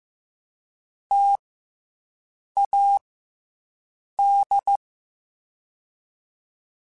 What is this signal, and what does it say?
Morse code, heard as a keyed tone, about 15 wpm, spaced as for 6 wpm, 800 Hz. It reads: TAD
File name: Difficult Combos / Difficult Combos-mp3s / T AOEI @13